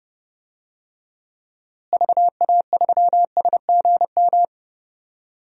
Morse code, heard as a keyed tone, about 32 wpm, 700 Hz. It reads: VA3SGM
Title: DXCC-100 Call Signs - 6 Characters - Repeated 2x - 30wpm